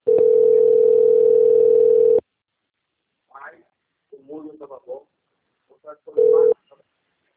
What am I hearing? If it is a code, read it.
Ringback tone (North America), 440 + 480 Hz, with speech.
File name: out- - - - - .88304